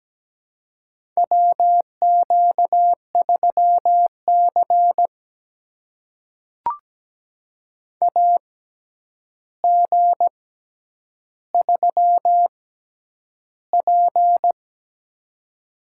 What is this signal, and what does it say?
Morse code, heard as a keyed tone, about 17 wpm, spaced as for 9 wpm, 700 Hz. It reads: WQ3C AG3P
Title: USA Call Signs - 4 Characters - 6x Character Spacing - 17wpm